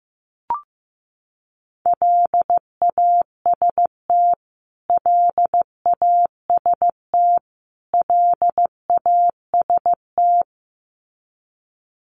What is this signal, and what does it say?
Morse code, heard as a keyed tone, about 15 wpm, 700 Hz. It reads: LAST LAST LAST
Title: Top 500 Words Repeated 15wpm